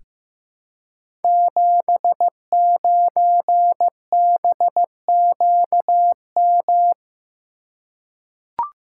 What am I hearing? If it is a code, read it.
Morse code, heard as a keyed tone, about 15 wpm, 700 Hz. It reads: 79BQM
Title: Code Groups - 5 Letter:Numbers - 15wpm